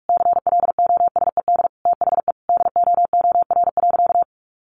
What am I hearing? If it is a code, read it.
Morse code, heard as a keyed tone, about 44 wpm, 700 Hz. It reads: CLOSED THE DOOR.
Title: Sentences from Top 1400 Words 45wpm